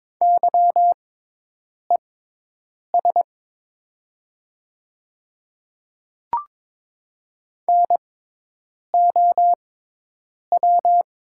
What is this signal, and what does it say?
Morse code, heard as a keyed tone, about 23 wpm, spaced as for 8 wpm, 700 Hz. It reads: YES NOW
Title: 6x Character Spacing - Top 500 Words - 22wpm